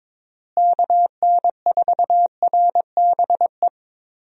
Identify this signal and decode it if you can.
Morse code, heard as a keyed tone, about 22 wpm, 700 Hz. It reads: KN4RBE
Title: USA Call Signs - 6 Characters - Repeated 3x - 22wpm